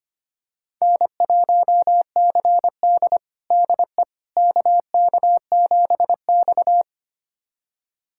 Morse code, heard as a keyed tone, about 25 wpm, 700 Hz. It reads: N1CD DE KK7X
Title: USA Callsign DE Callsign - 4 Characters - 25wpm